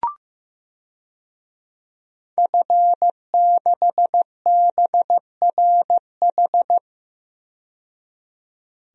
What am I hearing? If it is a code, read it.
Morse code, heard as a keyed tone, about 15 wpm, 700 Hz. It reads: F6BRH
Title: DXCC-100 Call Signs - 5 Characters - 15wpm